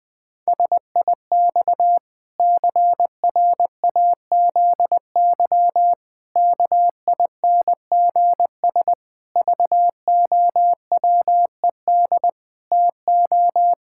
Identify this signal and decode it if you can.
Morse code, heard as a keyed tone, about 20 wpm, 700 Hz. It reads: SIX CRAZY KINGS VOWED TO
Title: Pangram Challenge v2 - 20wpm